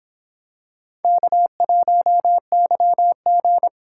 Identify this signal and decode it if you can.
Morse code, heard as a keyed tone, about 26 wpm, 700 Hz. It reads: K1YG